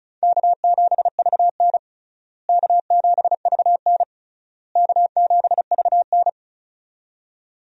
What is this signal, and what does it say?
Morse code, heard as a keyed tone, about 36 wpm, 700 Hz. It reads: K7VN K7VN K7VN
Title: USA Call Signs - 4 Characters - Repeated 3x - 35wpm